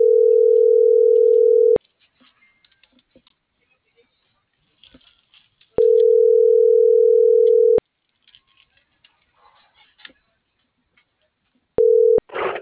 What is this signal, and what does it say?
Ringback tone (North America), 440 + 480 Hz.